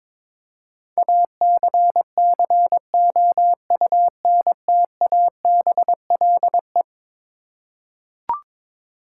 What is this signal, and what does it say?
Morse code, heard as a keyed tone, about 22 wpm, 700 Hz. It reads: ACCOUNTABLE
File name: practice / 22wpm / Eleven Letter Words - 22wpm